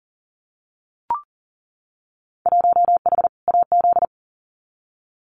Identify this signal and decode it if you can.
Morse code, heard as a keyed tone, about 39 wpm, 700 Hz. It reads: 1H AZ
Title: Winter Field Day Exchanges 40wpm